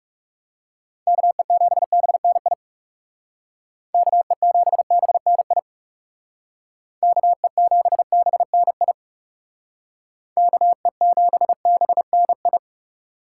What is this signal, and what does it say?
Morse code, heard as a keyed tone, about 36 wpm, 700 Hz. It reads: KE7BNI KE7BNI KE7BNI KE7BNI
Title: Speed Racing - USA 2x3 Call Signs 30wpm